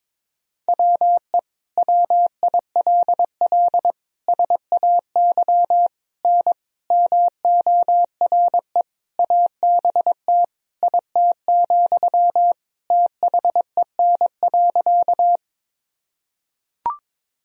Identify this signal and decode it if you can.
Morse code, heard as a keyed tone, about 21 wpm, 700 Hz. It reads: WE WILL SAY N MORE ABT IT, THEN.
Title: Sentences from Top 200 Words and Abbreviations 22wpm